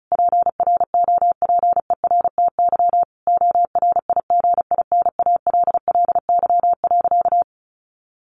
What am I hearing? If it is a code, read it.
Morse code, heard as a keyed tone, about 34 wpm, 700 Hz. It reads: PROPERTY ORIGINALLY.
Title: Sentences from Top 1500 Words 35wpm